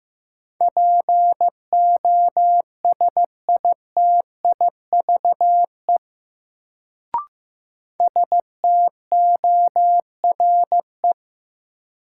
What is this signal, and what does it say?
Morse code, heard as a keyed tone, about 15 wpm, 700 Hz. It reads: POSITIVE STORE
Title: Sentences from Top 1000 Words - Review Part 5 - New Word Drill - 15wpm